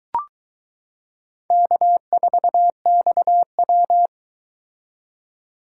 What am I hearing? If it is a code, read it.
Morse code, heard as a keyed tone, about 22 wpm, 700 Hz. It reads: K4XW